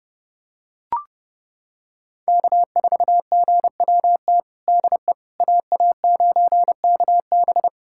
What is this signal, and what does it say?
Morse code, heard as a keyed tone, about 30 wpm, 700 Hz. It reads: K4GWT DE AA9KB
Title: USA Callsign DE Callsign 30wpm